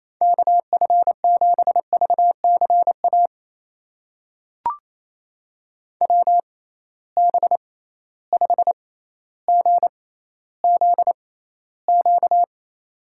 Morse code, heard as a keyed tone, about 28 wpm, spaced as for 14 wpm, 700 Hz. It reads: KF7VCA WB5GZQ
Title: USA Call Signs - 6 Characters - 6x Character Spacing - 28wpm